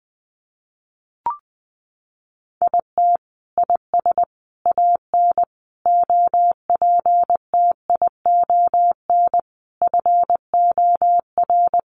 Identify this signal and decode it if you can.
Morse code, heard as a keyed tone, about 20 wpm, 700 Hz. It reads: IT IS AN OPTION FOR
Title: Sentences from Top 1300 Words 20wpm